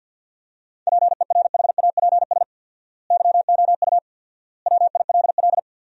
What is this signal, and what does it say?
Morse code, heard as a keyed tone, about 52 wpm, 700 Hz. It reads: PERHAPS YOU WILL
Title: 4x Word-Spacing - Sentences from Top 500 Words - 50wpm